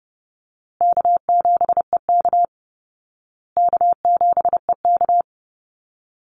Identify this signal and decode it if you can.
Morse code, heard as a keyed tone, about 30 wpm, 700 Hz. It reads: K7EK K7EK
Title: Easy Call Signs Repeated 3x - 30wpm